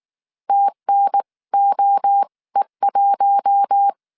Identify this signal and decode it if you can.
Morse code, heard as a keyed tone, about 19 wpm, spaced as for 16 wpm, 800 Hz. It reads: TNOE1
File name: CW / DET / 1 6 1500C S C 18wpm 2356